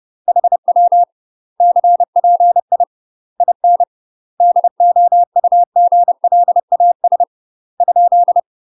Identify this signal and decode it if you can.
Morse code, heard as a keyed tone, about 30 wpm, 700 Hz. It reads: HW CPI IN DOUGLAS ?
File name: UK MASTER QSOs TRAINING / 30WPM / UK MASTER QSOs 0046